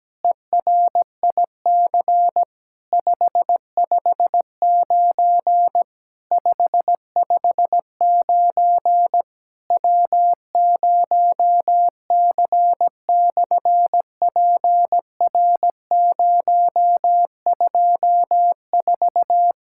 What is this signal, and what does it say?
Morse code, heard as a keyed tone, about 17 wpm, 700 Hz. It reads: ERIC 559 559 W0C/PR024
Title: SOTA QSO Practice 17wpm